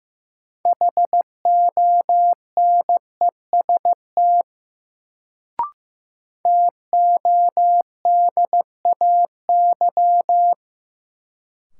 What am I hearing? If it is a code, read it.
Morse code, heard as a keyed tone, about 15 wpm, 700 Hz. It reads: HONEST TODAY
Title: Sets of 5 Words Encore - New Word Drill - 15wpm